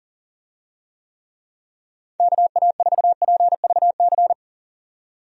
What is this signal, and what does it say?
Morse code, heard as a keyed tone, about 37 wpm, 700 Hz. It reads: KA4PVC
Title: USA Call Signs - 6 Characters - 40wpm